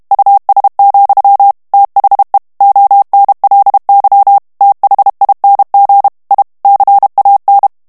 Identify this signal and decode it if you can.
Morse code, heard as a keyed tone, about 32 wpm, 800 Hz. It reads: US, THE ONLY THING I CAN
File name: ANGLAIS texte clair 2 vitesse 30 mots minute